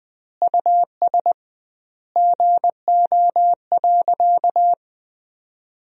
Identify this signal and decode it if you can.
Morse code, heard as a keyed tone, about 20 wpm, 700 Hz. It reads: US GO.
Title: 2x Word-Spacing - Sentences from Top 400 Words - 20wpm